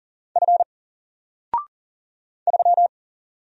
Morse code, heard as a keyed tone, about 40 wpm, 700 Hz. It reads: F 3
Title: Single Character (Letter:Number:Punctuation:Prosign) 40wpm